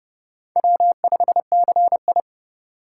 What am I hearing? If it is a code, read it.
Morse code, heard as a keyed tone, about 29 wpm, 700 Hz. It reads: W5CI